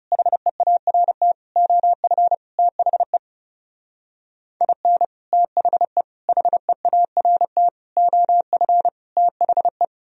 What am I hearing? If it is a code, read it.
Morse code, heard as a keyed tone, about 34 wpm, 700 Hz. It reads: HEART OF THE IN THE HEART OF THE
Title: Speed-Racing - Sets of 5 Words 30wpm